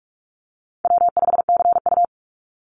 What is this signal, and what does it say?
Morse code, heard as a keyed tone, about 44 wpm, 700 Hz. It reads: W5CU
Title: Easy Call Signs Repeated 2x - 45wpm